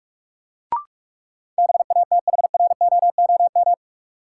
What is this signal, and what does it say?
Morse code, heard as a keyed tone, about 46 wpm, 700 Hz. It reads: BATHROOM
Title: Sets of 5 Words Encore - New Word Drill - 45wpm